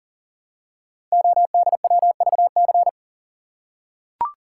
Morse code, heard as a keyed tone, about 39 wpm, 700 Hz. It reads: ODWVC
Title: Code Groups - 5 Letters - 40wpm